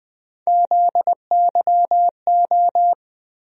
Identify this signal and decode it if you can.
Morse code, heard as a keyed tone, about 20 wpm, 700 Hz. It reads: ZYO
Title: USA Call Sign Postfixes - 3 Letters 20wpm